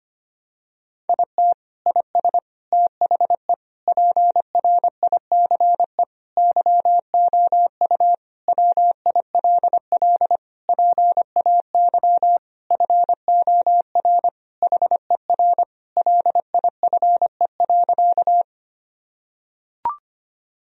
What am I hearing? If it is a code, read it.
Morse code, heard as a keyed tone, about 25 wpm, 700 Hz. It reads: IT IS THE PRICE YOU WILL PAY FOR HER LIFE.
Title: Sentences from Top 1000 Words - Review Part 4 - 25wpm